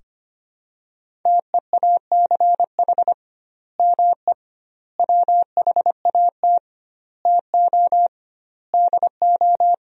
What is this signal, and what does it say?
Morse code, heard as a keyed tone, about 25 wpm, 700 Hz. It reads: TEACH ME WHAT TO DO
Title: 2x Word-Spacing - Sentences from Top 500 Words - 25wpm